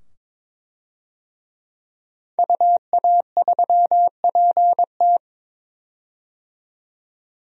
Morse code, heard as a keyed tone, about 22 wpm, 700 Hz. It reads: UA3PT
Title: DXCC-100 Call Signs - 5 Characters - 22wpm